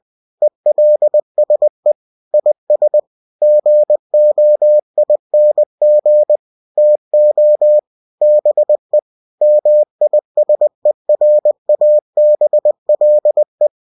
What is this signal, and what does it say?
Morse code, heard as a keyed tone, about 20 wpm, 600 Hz. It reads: ELSE IS GOING TO BE MISERABLE
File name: cw-books / winnie-the-pooh / winnie-20wpm-page-0048